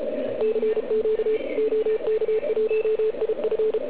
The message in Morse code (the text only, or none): CQ OK1IF